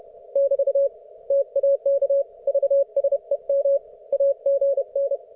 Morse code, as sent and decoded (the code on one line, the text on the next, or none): -...- / - .- -.- / ...- ... -- / .- --. -.
<BT> TAK VSEM AGN